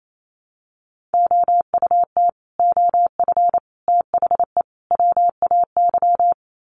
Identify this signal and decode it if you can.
Morse code, heard as a keyed tone, about 28 wpm, 700 Hz. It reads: OUT OF THE WAY
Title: Sets of 4 Words 28wpm